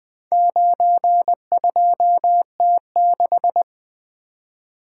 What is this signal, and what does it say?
Morse code, heard as a keyed tone, about 20 wpm, 700 Hz. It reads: 92T6
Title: Four Letter-Number Combinations - 20wpm